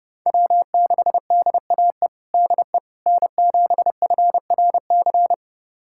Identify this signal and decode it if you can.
Morse code, heard as a keyed tone, about 32 wpm, 700 Hz. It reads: W6DAE DE N7FRC
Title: USA Callsign DE Callsign - 5 Characters - 30wpm